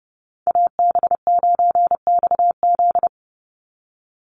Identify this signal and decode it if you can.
Morse code, heard as a keyed tone, about 30 wpm, 700 Hz. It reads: AB9XZ
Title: USA Call Signs 30wpm